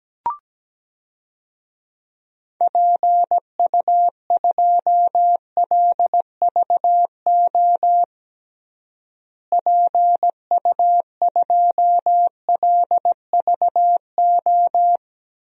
Morse code, heard as a keyed tone, about 17 wpm, 700 Hz. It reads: PU2LVO PU2LVO
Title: DXCC-100 Call Signs - 6 Characters - Repeated 3x - 17wpm